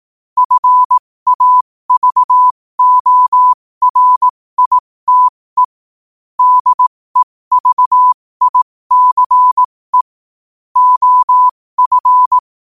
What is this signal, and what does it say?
Morse code, heard as a keyed tone, about 18 wpm, spaced as for 15 wpm, 1000 Hz. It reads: FAVORITE DEVICE OF
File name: qotdfiles / 15wpm